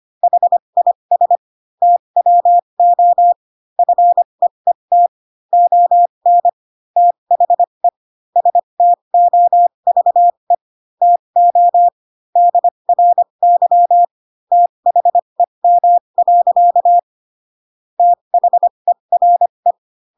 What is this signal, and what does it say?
Morse code, heard as a keyed tone, about 25 wpm, spaced as for 22 wpm, 700 Hz. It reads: HIS TWO FEET ON THE STOVE TO DRY THEM. THERE
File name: Pinocchio / Pinocchio ch6 18 25 ik1hge